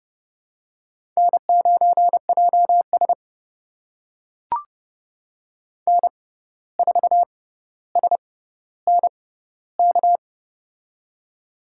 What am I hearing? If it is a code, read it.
Morse code, heard as a keyed tone, about 29 wpm, spaced as for 14 wpm, 700 Hz. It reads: N9JS N4SNK
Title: USA Call Signs - 6x Character Spacing - 30wpm